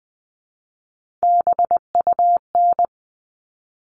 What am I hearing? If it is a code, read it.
Morse code, heard as a keyed tone, about 20 wpm, 700 Hz. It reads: BUN
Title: All Cut Numbers - 3-Digit 20wpm